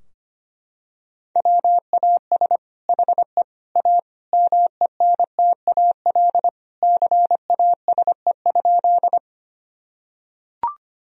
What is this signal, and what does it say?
Morse code, heard as a keyed tone, about 25 wpm, 700 Hz. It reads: WAS HE A MENTAL CASE?
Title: Sentences from Top 1000 Words - Review Part 4 - 25wpm